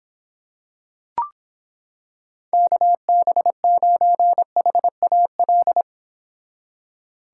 Morse code, heard as a keyed tone, about 26 wpm, 700 Hz. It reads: KB9HAL